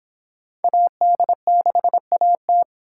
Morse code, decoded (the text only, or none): AD6AT